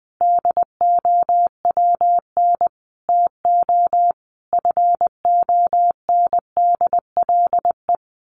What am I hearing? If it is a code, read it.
Morse code, heard as a keyed tone, about 20 wpm, 700 Hz. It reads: DOWN TO FONDLE
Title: A Sound Of Thunder 20wpm